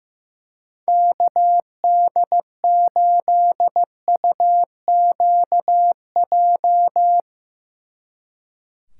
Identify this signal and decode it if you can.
Morse code, heard as a keyed tone, about 15 wpm, 700 Hz. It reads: KD8UQJ
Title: Speed Racing - USA 2x3 Call Signs 15wpm